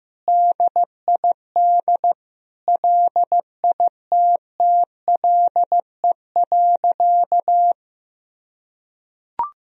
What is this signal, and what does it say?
Morse code, heard as a keyed tone, about 15 wpm, 700 Hz. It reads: DID LITTLE.
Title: Sentences from Top 1000 Words - Review Part 1 - 15wpm